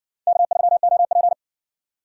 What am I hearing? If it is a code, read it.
Morse code, heard as a keyed tone, about 60 wpm, 700 Hz. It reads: K3OP